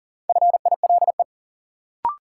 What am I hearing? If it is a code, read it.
Morse code, heard as a keyed tone, about 40 wpm, 700 Hz. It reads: FILE
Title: Sentences from Top 1000 Words - Review Part 4 - New Word Drill - 40wpm